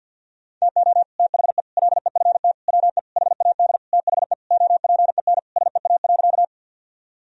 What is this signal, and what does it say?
Morse code, heard as a keyed tone, about 52 wpm, 700 Hz. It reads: TO THE LEFT WE HAD THE OPEN SEA.
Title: Sentences from Top 1000 Words - Review Part 1 - 50wpm